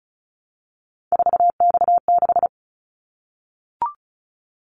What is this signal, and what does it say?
Morse code, heard as a keyed tone, about 36 wpm, 690 Hz. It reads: 4X6
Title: Callsign Prefixes for Top 50 DXCC - 35wpm